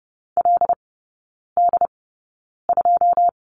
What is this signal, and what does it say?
Morse code, heard as a keyed tone, about 30 wpm, spaced as for 9 wpm, 700 Hz. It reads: LD2
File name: 081 Lesson 14e - Number 2 - Three Characters 30wpm